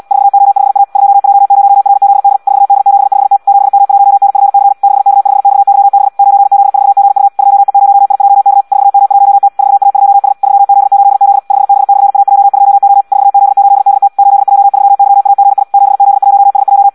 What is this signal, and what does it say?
Morse code, heard as a keyed tone, about 74 wpm, 800 Hz. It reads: 4O5A 2J0RZD 4D75T 7M1NFC 6K5XYB 9Z4GU 2E0IJK 4M1T 4A2S 3Z8Z 5B/AJ2O 4Z8DT 8J3YAGI 7Z1SJ
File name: Audio practica CW / qrq-tx-74 2